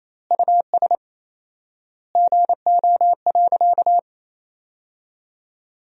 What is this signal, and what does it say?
Morse code, heard as a keyed tone, about 28 wpm, 700 Hz. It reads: US GO.
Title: 4x Word-Spacing - Sentences from Top 400 Words - 28wpm